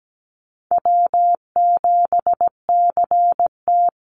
Morse code, heard as a keyed tone, about 17 wpm, 700 Hz. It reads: W7CT
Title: Easy Call Signs - 17wpm